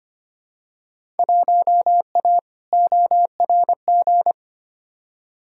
Morse code, heard as a keyed tone, about 25 wpm, 700 Hz. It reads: 1A ORG
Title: Field Day Exchanges Encore - 25wpm